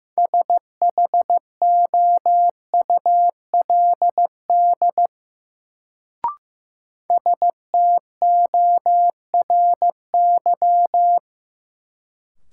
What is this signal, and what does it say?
Morse code, heard as a keyed tone, about 15 wpm, 700 Hz. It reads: SHOULD STORY
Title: Sentences from Top 300 Words - New Word Drill - 15wpm